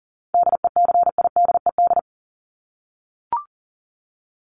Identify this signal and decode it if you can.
Morse code, heard as a keyed tone, about 40 wpm, 700 Hz. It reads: DECIDED
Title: ED Suffix - Top 100 Words with Base Words 40wpm